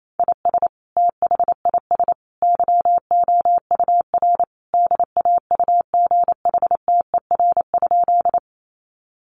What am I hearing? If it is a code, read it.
Morse code, heard as a keyed tone, about 28 wpm, 700 Hz. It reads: IS THIS YOUR DAUGHTER?
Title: Sentences from Top 1500 Words 28wpm